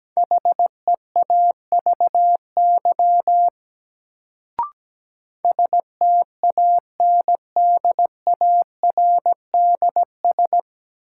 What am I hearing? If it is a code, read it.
Morse code, heard as a keyed tone, about 17 wpm, 700 Hz. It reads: HEAVY STANDARDS
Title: Sentences from Top 1000 Words - Review Part 6 - New Word Drill - 17wpm